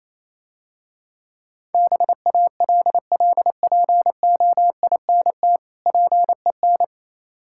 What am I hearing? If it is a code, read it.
Morse code, heard as a keyed tone, about 27 wpm, 700 Hz. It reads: BALLPOINT PEN
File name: Easy Words - Speed-Racing with Hints - 22wpm